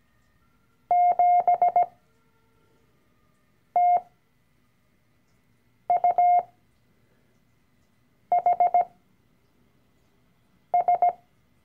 Morse code, heard as a keyed tone, about 17 wpm, spaced as for 4 wpm, 710 Hz. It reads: 7TUHS
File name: CW-Online-Kurs / Anfänger / Lektion 012